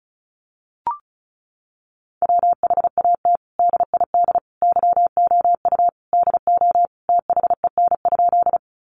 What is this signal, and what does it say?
Morse code, heard as a keyed tone, about 34 wpm, 700 Hz. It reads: WHAT DID YOU DO THEN?
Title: Sentences from Top 100 Words 35wpm